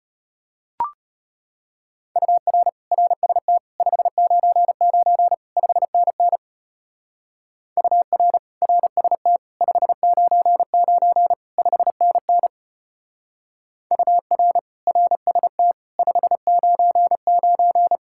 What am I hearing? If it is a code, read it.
Morse code, heard as a keyed tone, about 32 wpm, 700 Hz. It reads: UR RST 599 5NN UR RST 599 5NN UR RST 599